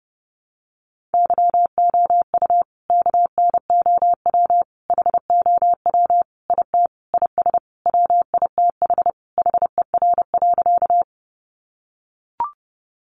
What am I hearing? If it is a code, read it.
Morse code, heard as a keyed tone, about 30 wpm, 700 Hz. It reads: YOU KNOW HOW IT IS WITH HER.
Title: Sentences from Top 100 Words 30wpm